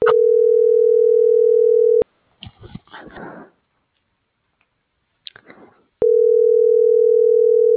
Ringback tone (North America), 440 + 480 Hz.